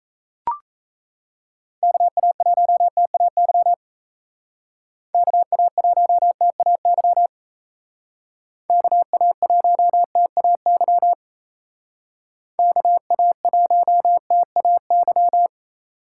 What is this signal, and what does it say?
Morse code, heard as a keyed tone, about 33 wpm, 700 Hz. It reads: KA1TAY KA1TAY KA1TAY KA1TAY
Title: Speed Racing - USA 2x3 Call Signs 28wpm